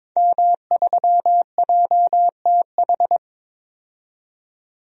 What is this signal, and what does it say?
Morse code, heard as a keyed tone, about 22 wpm, 700 Hz. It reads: M3JTH
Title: DXCC-100 Call Signs - 5 Characters - 22wpm